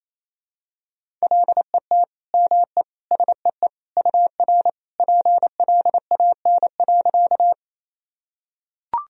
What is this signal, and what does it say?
Morse code, heard as a keyed tone, about 28 wpm, 700 Hz. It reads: LET ME SEE UR PLAN.
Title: Sentences from Top 500 Words and Abbreviations 28wpm